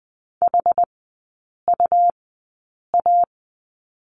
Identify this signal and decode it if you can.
Morse code, heard as a keyed tone, about 20 wpm, spaced as for 8 wpm, 700 Hz. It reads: HUA
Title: 100 Lesson 17e - Letter u - Three Characters 20wpm